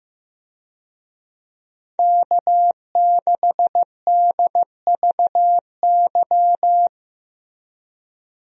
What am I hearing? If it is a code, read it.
Morse code, heard as a keyed tone, about 15 wpm, 700 Hz. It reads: K6DVY
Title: USA Call Signs - 5 Characters - Repeated 3x - 15wpm